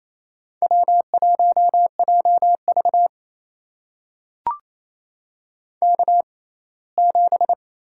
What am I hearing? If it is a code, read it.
Morse code, heard as a keyed tone, about 28 wpm, spaced as for 19 wpm, 700 Hz. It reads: W1JV K7
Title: USA Call Signs - 4 Characters - 6x Character Spacing - 28wpm